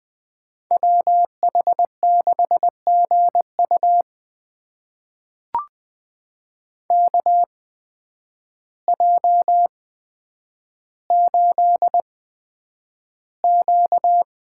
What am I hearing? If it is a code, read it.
Morse code, heard as a keyed tone, about 20 wpm, spaced as for 9 wpm, 700 Hz. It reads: WH6GU KJ8Q
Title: USA Call Signs - 8x Character Spacing - 20wpm